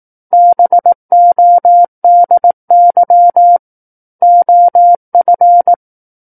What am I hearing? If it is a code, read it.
Morse code, heard as a keyed tone, about 18 wpm, 700 Hz. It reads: BODY OF
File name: 18 WPM